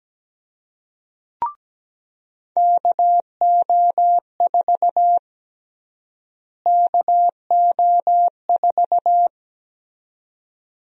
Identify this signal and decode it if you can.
Morse code, heard as a keyed tone, about 17 wpm, 700 Hz. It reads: KO4 KO4
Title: USA Call Sign Prefixes - Repeated 2x - 17wpm